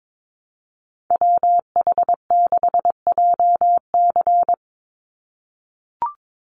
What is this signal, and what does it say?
Morse code, heard as a keyed tone, about 22 wpm, 700 Hz. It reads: WH6JC
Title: USA Call Signs 22wpm